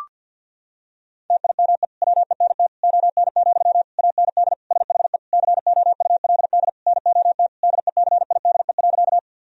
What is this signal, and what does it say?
Morse code, heard as a keyed tone, about 52 wpm, 700 Hz. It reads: TIME WENT ON, AND SHE COULD NOT DECIDE.
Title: Sentences from Top 1000 Words - Review Part 3 - 50wpm